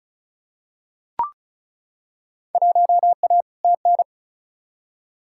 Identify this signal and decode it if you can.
Morse code, heard as a keyed tone, about 34 wpm, 700 Hz. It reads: 1A TN
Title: Field Day Exchanges - 35wpm